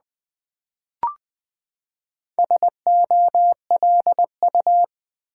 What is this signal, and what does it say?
Morse code, heard as a keyed tone, about 20 wpm, 700 Hz. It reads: SOLU
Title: Build A Word - 4 Letters At A Time - 8-Letter Words - 20wpm